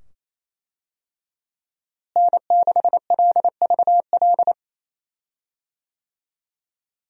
Morse code, heard as a keyed tone, about 28 wpm, 700 Hz. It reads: N6LVL
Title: USA Call Signs - 5 Characters - Repeated 2x - 28wpm